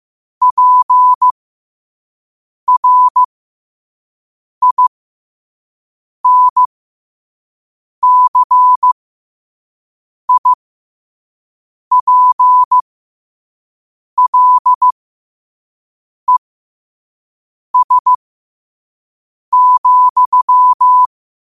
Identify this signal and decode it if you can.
Morse code, heard as a keyed tone, about 15 wpm, spaced as for 5 wpm, 1000 Hz. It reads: PRINCIPLES,
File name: qotdfiles / 5wpm